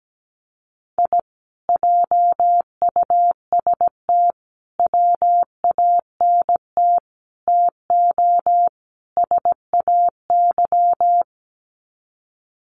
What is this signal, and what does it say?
Morse code, heard as a keyed tone, about 17 wpm, 700 Hz. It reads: I JUST WANT TO SAY
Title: Sets of 5 Words Encore 17wpm